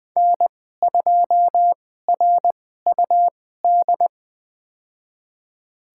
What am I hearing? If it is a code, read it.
Morse code, heard as a keyed tone, about 20 wpm, spaced as for 14 wpm, 700 Hz. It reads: N2RUD
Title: USA Call Signs - 2x Character Spacing - 20wpm